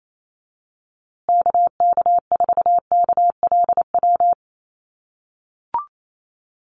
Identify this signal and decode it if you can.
Morse code, heard as a keyed tone, about 28 wpm, 700 Hz. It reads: KK4KLW
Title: USA Call Signs Repeated 3x 28wpm